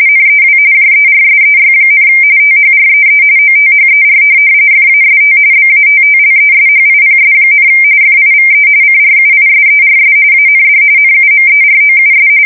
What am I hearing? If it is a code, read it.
RTTY, about 45.45 baud. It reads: NEW JERSEY, THREE OTHER FAST BATTLESHIPS, FIVE CRUISERS AND NUMEROUS
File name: fsk170 / USS-New-Jersey